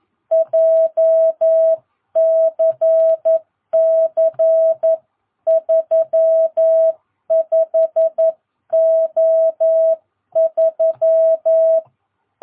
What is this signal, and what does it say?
Morse code, heard as a keyed tone, about 11 wpm, 660 Hz. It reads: JCC 35O3
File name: jh8cbh / 2022cw